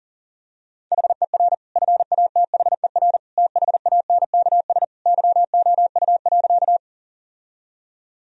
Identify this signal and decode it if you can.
Morse code, heard as a keyed tone, about 37 wpm, 700 Hz. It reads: HER FATHER THANKS YOU.
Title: Sentences from Top 1000 Words - Review Part 3 - 40wpm